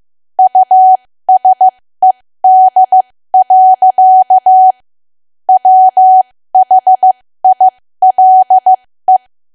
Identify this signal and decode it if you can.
Morse code, heard as a keyed tone, about 15 wpm, spaced as for 13 wpm, 750 Hz. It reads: USED. WHILE
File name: cwa / W1AW-Level2 / 13WPM